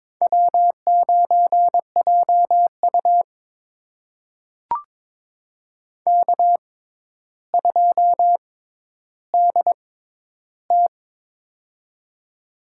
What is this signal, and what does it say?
Morse code, heard as a keyed tone, about 22 wpm, spaced as for 11 wpm, 700 Hz. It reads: W9JU K2DT
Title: USA Call Signs - 4 Characters - 6x Character Spacing - 22wpm